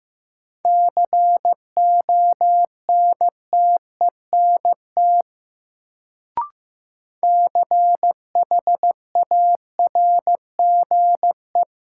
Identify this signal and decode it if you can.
Morse code, heard as a keyed tone, about 15 wpm, 700 Hz. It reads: CONTENT CHARGE
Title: Sentences from Top 900 Words - New Word Drill - 15wpm